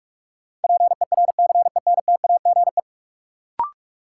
Morse code, heard as a keyed tone, about 46 wpm, 700 Hz. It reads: PERCENTAGE
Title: Sentences from Top 1300 Words - New Word Drill Alt - 45wpm